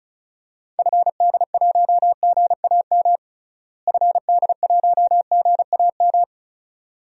Morse code, heard as a keyed tone, about 34 wpm, 700 Hz. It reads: FD1GAM FD1GAM
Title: DXCC-100 Call Signs - 6 Characters - Repeated 2x - 35wpm